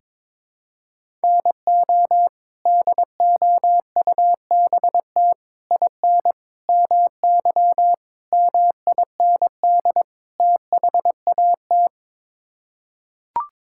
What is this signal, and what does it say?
Morse code, heard as a keyed tone, about 22 wpm, 700 Hz. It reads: NO DOUBT IN MY MIND THAT
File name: practice / 22wpm / Sets of 6 Words - 22wpm